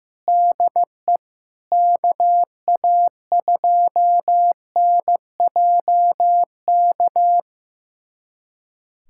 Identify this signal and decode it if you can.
Morse code, heard as a keyed tone, about 16 wpm, 700 Hz. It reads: DE KA2NJK
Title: USA Callsign DE Callsign - 6 Characters - 15wpm